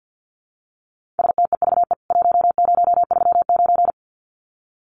Morse code, heard as a keyed tone, about 48 wpm, 700 Hz. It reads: STEVE 1039
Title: CWT Exchange 50wpm